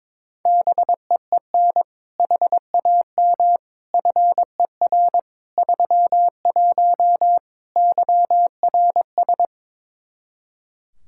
Morse code, heard as a keyed tone, about 22 wpm, 700 Hz. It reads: BEEN HAM FER 31 YRS
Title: K1USN SST QSO Practice 22wpm